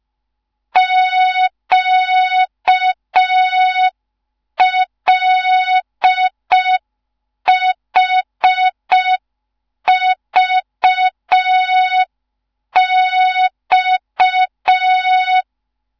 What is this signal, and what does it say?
Morse code, heard as a keyed tone, about 5 wpm, 3700 Hz. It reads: QLHVX
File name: Morze-HA2MN / hang / 15 GYAKORLAT